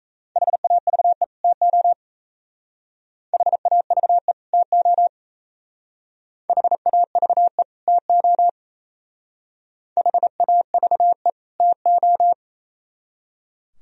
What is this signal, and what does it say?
Morse code, heard as a keyed tone, about 35 wpm, 690 Hz. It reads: HAVE TO HAVE TO HAVE TO HAVE TO
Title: Speed-Racing - Sets of 2 Words 28wpm